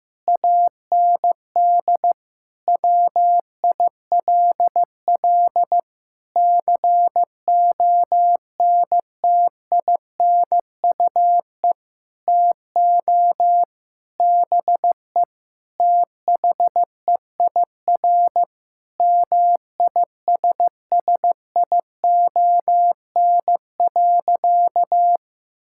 Morse code, heard as a keyed tone, about 15 wpm, 700 Hz. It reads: AND WILL CONTINUE TO BE THEIR MISSION.
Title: Sentences from Top 1600 Words 15wpm